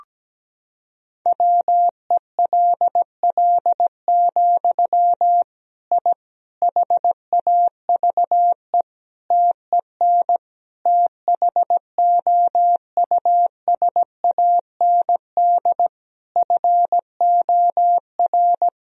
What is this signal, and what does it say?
Morse code, heard as a keyed tone, about 17 wpm, 700 Hz. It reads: WELL, I HAVE TEN THOUSAND FOR
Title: Sentences from Top 1000 Words - Review Part 3 - 17wpm